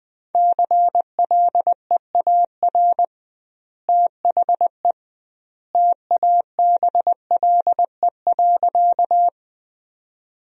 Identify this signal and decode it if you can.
Morse code, heard as a keyed tone, about 20 wpm, 700 Hz. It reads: CLEAR THE TABLE.
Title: 2x Word-Spacing - Sentences from Top 500 Words - 20wpm